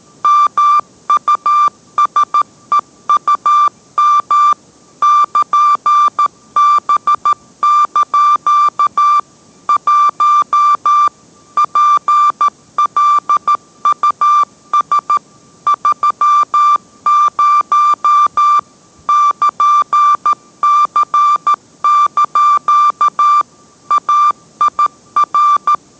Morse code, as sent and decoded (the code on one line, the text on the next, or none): -- ..- ... ..- -- / -.--. -... -.--.- / .---- / .--. .-.. ..- ... / ...-- ----- / -.--. -.-. -.--.- / .- .. .-.
MUSEUM <KN>B) 1 PLUS 30 <KN>C) AIR